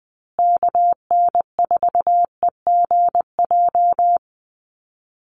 Morse code, heard as a keyed tone, about 20 wpm, 700 Hz. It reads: KN4EGJ
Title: USA Call Signs Repeated 3x 20wpm